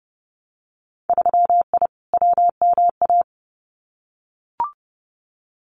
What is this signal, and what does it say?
Morse code, heard as a keyed tone, about 31 wpm, 700 Hz. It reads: 3I WMA
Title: Winter Field Day Exchanges 30wpm